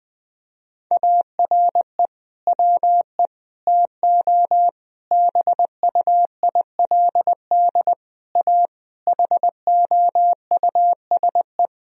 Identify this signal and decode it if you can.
Morse code, heard as a keyed tone, about 20 wpm, 700 Hz. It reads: ARE WE TO BUILD A HOUSE
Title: Sentences from Top 1000 Words - Review Part 1 - 20wpm